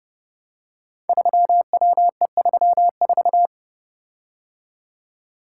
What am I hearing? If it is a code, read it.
Morse code, heard as a keyed tone, about 30 wpm, 700 Hz. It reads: 3WE34
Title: Code Groups - 5 Letter:Numbers - 30wpm